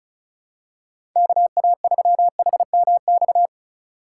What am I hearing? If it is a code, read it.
Morse code, heard as a keyed tone, about 34 wpm, 700 Hz. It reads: KA3HMX
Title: USA Call Signs - 6 Characters - 3x Character Spacing - 35wpm